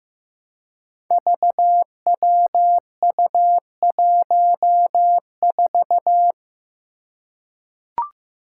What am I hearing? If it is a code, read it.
Morse code, heard as a keyed tone, about 15 wpm, 700 Hz. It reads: VWU14
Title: Five Challenging Letter-Number Combinations 15wpm